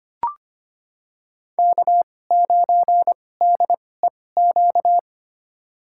Morse code, heard as a keyed tone, about 25 wpm, spaced as for 18 wpm, 700 Hz. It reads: K9DEQ
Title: USA Call Signs - 5 Characters - 2x Character Spacing - 25wpm